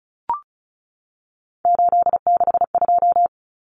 Morse code, heard as a keyed tone, about 34 wpm, 700 Hz. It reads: 862
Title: Numbers - 3-Digit 35wpm